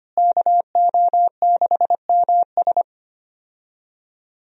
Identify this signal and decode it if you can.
Morse code, heard as a keyed tone, about 25 wpm, 700 Hz. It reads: KO6MS
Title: USA Call Signs - 5 Characters - Repeated 2x - 25wpm